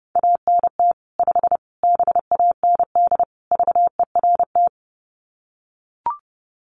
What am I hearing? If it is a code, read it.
Morse code, heard as a keyed tone, about 30 wpm, 700 Hz. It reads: ANT 5 BAND VERT
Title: Common QSO Phrases 30wpm